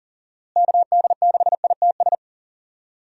Morse code, heard as a keyed tone, about 40 wpm, 700 Hz. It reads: KD6ITS
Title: USA Call Signs - 6 Characters - Repeated 2x - 40wpm